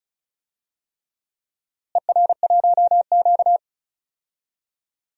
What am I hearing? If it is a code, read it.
Morse code, heard as a keyed tone, about 36 wpm, 700 Hz. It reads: ER1Q
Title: DXCC-100 Call Signs - 4 Characters - Repeated 3x - 35wpm